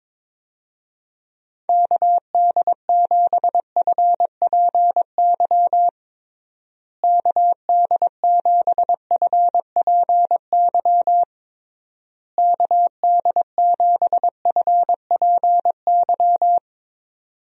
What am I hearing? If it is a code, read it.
Morse code, heard as a keyed tone, about 22 wpm, 700 Hz. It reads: KD7FPY KD7FPY KD7FPY
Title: USA Call Signs - 6 Characters - Repeated 3x - 22wpm